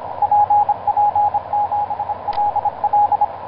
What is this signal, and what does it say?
Morse code, heard as a keyed tone, about 26 wpm, 830 Hz, with radio noise. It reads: PP7LL
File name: zp6cw ik1rgk etc 12m cw 19feb2010 1640